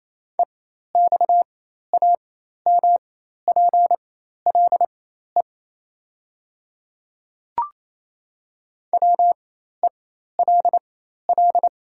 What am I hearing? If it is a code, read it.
Morse code, heard as a keyed tone, about 27 wpm, spaced as for 13 wpm, 700 Hz. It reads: EXAMPLE WELL
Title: 4x Character Spacing - Top 300 Words - 28wpm